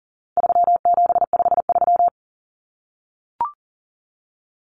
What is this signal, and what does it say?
Morse code, heard as a keyed tone, about 37 wpm, 700 Hz. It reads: 3753
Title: Common Cut Numbers - 4-Digit 40wpm